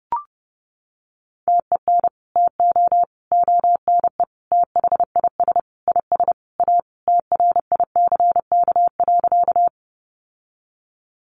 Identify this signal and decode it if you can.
Morse code, heard as a keyed tone, about 30 wpm, 700 Hz. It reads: TEN TO ONE THIS IS A TRICK.
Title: Sentences from Top 1500 Words 30wpm